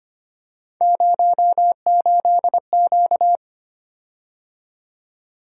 Morse code, heard as a keyed tone, about 25 wpm, 700 Hz. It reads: 08Q
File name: Three Letter-Number Combinations - 25wpm